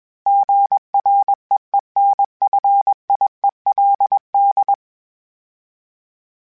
Morse code, heard as a keyed tone, about 20 wpm, 800 Hz. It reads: GREENFIELD
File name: fromSMs / FromSMs mp3s / Cities@20